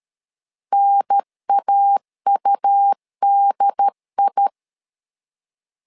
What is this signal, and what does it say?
Morse code, heard as a keyed tone, about 13 wpm, 800 Hz. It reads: NAUDI
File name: LEZIONI / 3 MWRO / 3 M 12wpm 1541